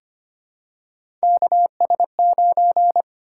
Morse code, heard as a keyed tone, about 25 wpm, 700 Hz. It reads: KS9
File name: Three Letter-Number Combinations - 25wpm